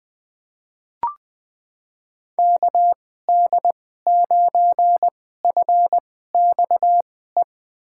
Morse code, heard as a keyed tone, about 20 wpm, spaced as for 15 wpm, 700 Hz. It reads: KD9FXE
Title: USA Call Signs - 2x Character Spacing - 20wpm